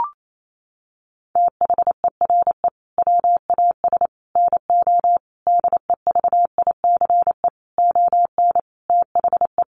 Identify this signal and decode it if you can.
Morse code, heard as a keyed tone, about 28 wpm, 700 Hz. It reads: THERE WAS NO DEVICE ON THE
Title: Sentences from Top 1500 Words 28wpm